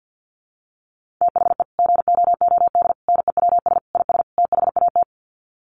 Morse code, heard as a keyed tone, about 48 wpm, 700 Hz. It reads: THE GOOD NEWS IS THAT
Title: Sets of 5 Words 50wpm